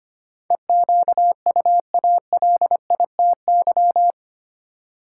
Morse code, heard as a keyed tone, about 25 wpm, 700 Hz. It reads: EQUALITY
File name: 8 Letter Words - First 3 Letters Given - 25wpm